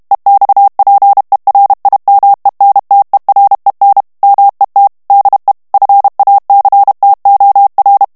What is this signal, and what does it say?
Morse code, heard as a keyed tone, about 32 wpm, 800 Hz. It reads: EXPERIMENTEREN MET DE FACTOR
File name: morse mp3 / HOLLANDAIS texte clair 1 vitesse 30 mots minute